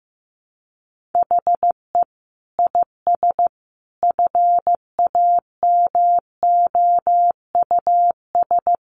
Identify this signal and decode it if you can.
Morse code, heard as a keyed tone, about 15 wpm, 700 Hz. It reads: HE IS FAMOUS
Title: Sentences from Top 1500 Words 15wpm